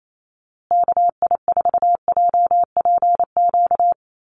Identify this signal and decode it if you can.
Morse code, heard as a keyed tone, about 28 wpm, 700 Hz. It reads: KI4JPQ
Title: USA Call Signs Repeated 3x 28wpm